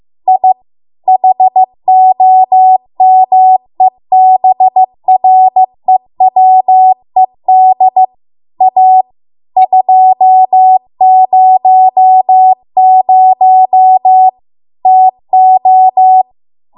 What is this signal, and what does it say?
Morse code, heard as a keyed tone, about 15 wpm, 750 Hz. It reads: I HOMEBREWED A 200 TO